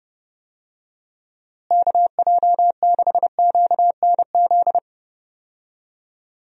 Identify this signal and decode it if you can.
Morse code, heard as a keyed tone, about 30 wpm, 700 Hz. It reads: KJ6QNZ
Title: USA Call Signs - 6 Characters - Repeated 2x - 30wpm